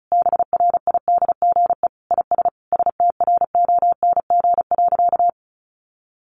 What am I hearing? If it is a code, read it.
Morse code, heard as a keyed tone, about 34 wpm, 700 Hz. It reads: BRIDGE IS STRONG.
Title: Sentences from Top 1400 Words 35wpm